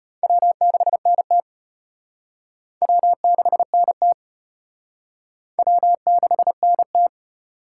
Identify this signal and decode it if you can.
Morse code, heard as a keyed tone, about 33 wpm, 700 Hz. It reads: W6NT W6NT W6NT